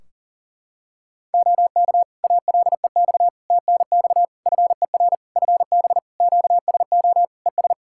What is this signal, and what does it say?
Morse code, heard as a keyed tone, about 44 wpm, 700 Hz. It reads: OK ALEX TNX FER FB QSO ES
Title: Practice QSO - Exchange 4 - Ending - 40wpm